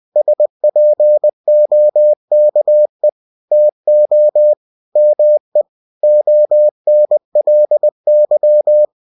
Morse code, heard as a keyed tone, about 20 wpm, 600 Hz. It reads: SPOKE TO ME ONLY
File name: cw-books / winnie-the-pooh / winnie-20wpm-page-0108